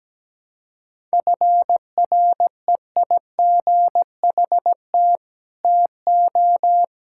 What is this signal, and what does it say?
Morse code, heard as a keyed tone, about 17 wpm, 700 Hz. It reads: FREIGHT TO
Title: Pangram Challenge v2 - 17wpm